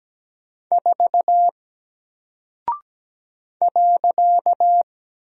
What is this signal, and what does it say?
Morse code, heard as a keyed tone, about 17 wpm, 700 Hz. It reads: 4 .
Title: Single Character (Letter:Number:Punctuation:Prosign) Alt 17wpm